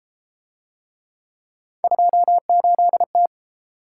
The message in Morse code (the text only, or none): 28T